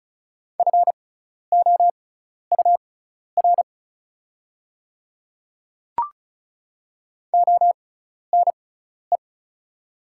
Morse code, heard as a keyed tone, about 36 wpm, spaced as for 12 wpm, 690 Hz. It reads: FOUR ONE
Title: 6x Character Spacing - Top 300 Words - 35wpm